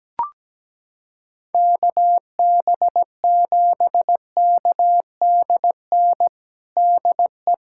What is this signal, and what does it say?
Morse code, heard as a keyed tone, about 17 wpm, 700 Hz. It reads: KB7KDN DE
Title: USA Callsign DE Callsign - 6 Characters - 17wpm